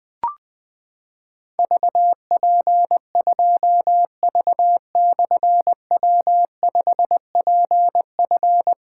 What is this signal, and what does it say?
Morse code, heard as a keyed tone, about 21 wpm, 700 Hz. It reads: VP2V/W5PF
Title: Difficult Call Signs 20wpm